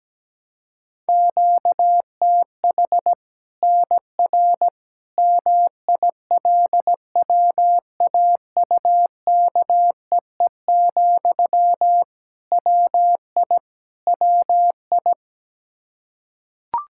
Morse code, heard as a keyed tone, about 17 wpm, 700 Hz. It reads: QTH NR MILWAUKEE, WI WI
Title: K1USN SST QSO Practice 17wpm